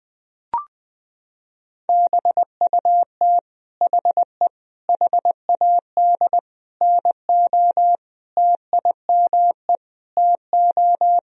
Morse code, heard as a keyed tone, about 20 wpm, 700 Hz. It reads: BUT HE HAD NO TIME TO
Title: Sentences from Top 1000 Words - Review Part 1 - 20wpm